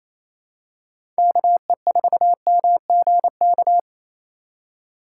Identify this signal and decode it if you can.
Morse code, heard as a keyed tone, about 27 wpm, 700 Hz. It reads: KE4MGK
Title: USA Call Signs - 6 Characters - 8x Character Spacing - 28wpm